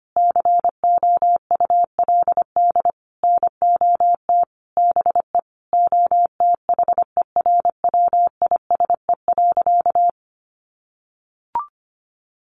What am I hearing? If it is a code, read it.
Morse code, heard as a keyed tone, about 25 wpm, 700 Hz. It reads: COULD NOT BE OTHERWISE.
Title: Sentences from Top 1200 Words 25wpm